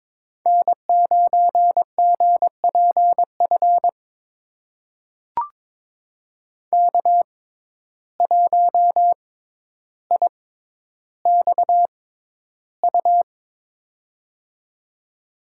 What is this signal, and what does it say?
Morse code, heard as a keyed tone, about 22 wpm, spaced as for 11 wpm, 700 Hz. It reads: N9GPF K1IXU
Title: USA Call Signs - 5 Characters - 6x Character Spacing - 22wpm